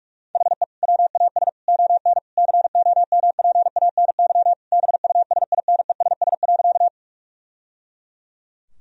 Morse code, heard as a keyed tone, about 46 wpm, 700 Hz. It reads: HE WAS ON COMPANY BUSINESS.
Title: Sentences from Top 1000 Words - Review Part 3 - 45wpm